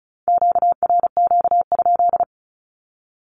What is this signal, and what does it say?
Morse code, heard as a keyed tone, about 36 wpm, 700 Hz. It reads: QRQ?
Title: Q-Codes with Spoken Meaning - 35wpm